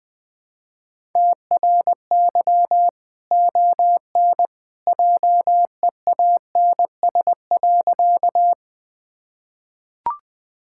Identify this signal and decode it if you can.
Morse code, heard as a keyed tone, about 20 wpm, 700 Hz. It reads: TRY ON JEANS.
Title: Easy Sentences with Hints - 20wpm